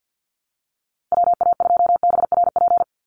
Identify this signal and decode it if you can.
Morse code, heard as a keyed tone, about 49 wpm, 700 Hz. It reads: WA2BRP
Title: USA Call Signs 50wpm